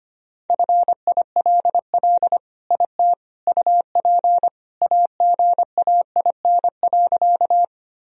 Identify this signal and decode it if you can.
Morse code, heard as a keyed tone, about 25 wpm, 700 Hz. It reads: FILL IT UP AGAIN.
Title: Sentences from Top 1000 Words - Review Part 3 - 25wpm